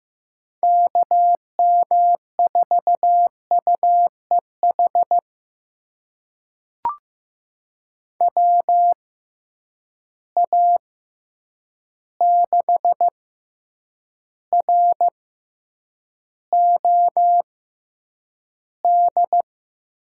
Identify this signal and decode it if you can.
Morse code, heard as a keyed tone, about 15 wpm, spaced as for 8 wpm, 700 Hz. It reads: KM4UEH WA6ROD
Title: USA Call Signs - 6 Characters - 6x Character Spacing - 15wpm